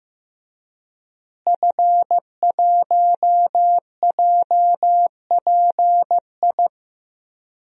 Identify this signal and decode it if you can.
Morse code, heard as a keyed tone, about 15 wpm, 700 Hz. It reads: F1JPI
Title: DXCC-100 Call Signs - 5 Characters - Repeated 2x - 15wpm